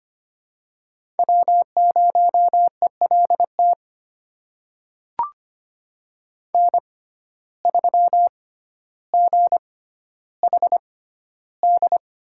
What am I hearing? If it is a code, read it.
Morse code, heard as a keyed tone, about 25 wpm, spaced as for 13 wpm, 700 Hz. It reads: W0ELT N3GHD
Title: USA Call Signs - 5 Characters - 6x Character Spacing - 25wpm